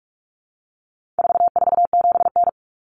Morse code, heard as a keyed tone, about 44 wpm, 700 Hz. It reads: 447N
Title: Common Cut Numbers - 4-Digit 45wpm